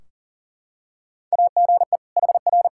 Morse code, heard as a keyed tone, about 40 wpm, 700 Hz. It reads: AGE HR
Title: Practice QSO - Exchange 3 - More Info - 40wpm